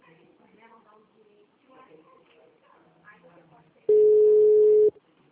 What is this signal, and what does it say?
Ringback tone (Europe, 425 Hz), with speech.
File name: out- -2016- - - .17273